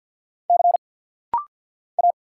Morse code, heard as a keyed tone, about 46 wpm, 700 Hz. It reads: C A
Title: Single Letters - Rapid-Fire 50wpm